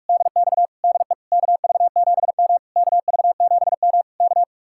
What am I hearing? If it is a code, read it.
Morse code, heard as a keyed tone, about 44 wpm, 700 Hz. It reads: DX DE KV7M KV7M K